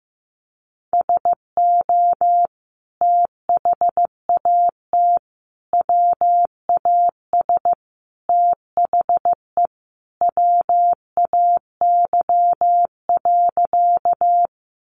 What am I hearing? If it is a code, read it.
Morse code, heard as a keyed tone, about 15 wpm, 700 Hz. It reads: SO THAT WAS THE WAY.
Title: Sentences from Top 100 Words 15wpm